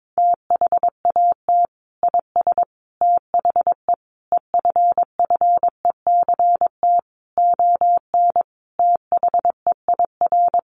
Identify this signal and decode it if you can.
Morse code, heard as a keyed tone, about 22 wpm, 700 Hz. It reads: THAT IS THE EFFECT ON THEIR
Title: Sentences from Top 1300 Words 22wpm